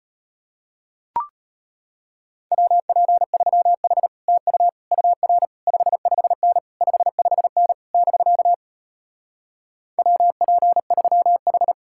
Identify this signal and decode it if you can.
Morse code, heard as a keyed tone, about 33 wpm, 700 Hz. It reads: WP3H TU UR 55N 55N <BK> WP3H